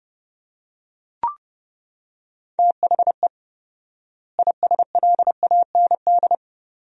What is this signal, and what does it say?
Morse code, heard as a keyed tone, about 30 wpm, 700 Hz. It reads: THE ISLAND
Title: 4x Word-Spacing - Sentences from Top 500 Words - 30wpm